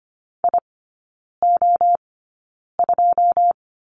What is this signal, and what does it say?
Morse code, heard as a keyed tone, about 25 wpm, spaced as for 9 wpm, 690 Hz. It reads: IO2
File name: Lesson 15e - Number 5 - Three Characters 25wpm